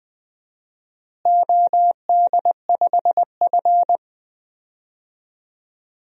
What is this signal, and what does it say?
Morse code, heard as a keyed tone, about 20 wpm, 700 Hz. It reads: OD5F
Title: DXCC-100 Call Signs - 4 Characters - Repeated 3x - 20wpm